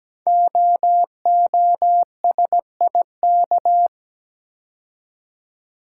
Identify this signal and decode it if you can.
Morse code, heard as a keyed tone, about 17 wpm, 700 Hz. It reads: OOSIK
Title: Code Groups - 5 Letters - 17wpm